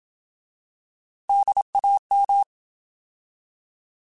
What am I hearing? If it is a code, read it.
Morse code, heard as a keyed tone, about 27 wpm, 800 Hz. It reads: DAM